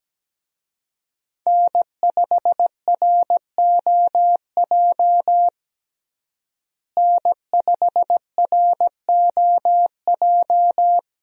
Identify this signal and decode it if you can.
Morse code, heard as a keyed tone, about 17 wpm, 700 Hz. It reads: N5ROJ N5ROJ
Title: USA Call Signs - 5 Characters - Repeated 2x - 17wpm